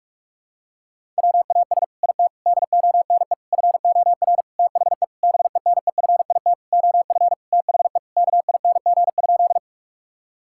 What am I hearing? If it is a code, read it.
Morse code, heard as a keyed tone, about 46 wpm, 700 Hz. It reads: WAS IT DONE FOR THE BENEFIT OF THE KING?
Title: Sentences from Top 1000 Words - Review Part 6 - 45wpm